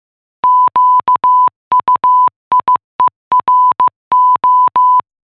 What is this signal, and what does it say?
Morse code, heard as a keyed tone, about 15 wpm, 1000 Hz. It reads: QUIERO